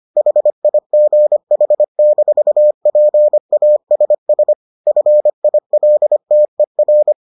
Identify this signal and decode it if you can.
Morse code, heard as a keyed tone, about 25 wpm, 600 Hz. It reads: HIGH-PASS FILTER